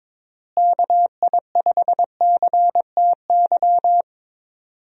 Morse code, heard as a keyed tone, about 22 wpm, 700 Hz. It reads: KI5CTY
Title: USA Call Signs - 6 Characters - Repeated 3x - 22wpm